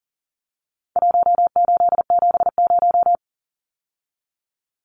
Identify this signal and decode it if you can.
Morse code, heard as a keyed tone, about 40 wpm, 700 Hz. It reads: 1870
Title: Numbers - Contest Style - 4-Digit 40wpm